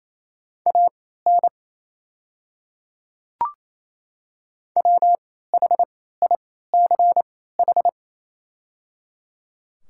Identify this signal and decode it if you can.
Morse code, heard as a keyed tone, about 27 wpm, spaced as for 16 wpm, 700 Hz. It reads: AN WHICH
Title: 3x Character Spacing - Top 100 Words - 28wpm